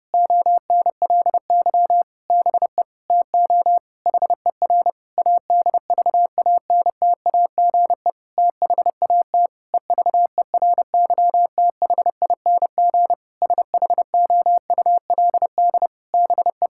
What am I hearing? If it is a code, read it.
Morse code, heard as a keyed tone, about 30 wpm, 700 Hz. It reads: ONLY BE TO HER ADVANTAGE THAT EVERYTHING SHOULD BE
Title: Sentences from Top 1000 Words - Review Part 6 - 30wpm